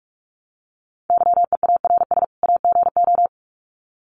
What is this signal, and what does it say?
Morse code, heard as a keyed tone, about 46 wpm, 700 Hz. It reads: YEARS AGO
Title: Sets of 2 Words 45wpm